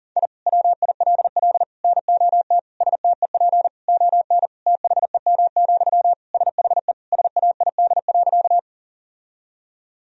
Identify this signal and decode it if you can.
Morse code, heard as a keyed tone, about 40 wpm, 700 Hz. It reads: I WILL NOT STEP ON THEM, SHE SAID.
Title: Sentences from Top 1000 Words - Review Part 2 - 40wpm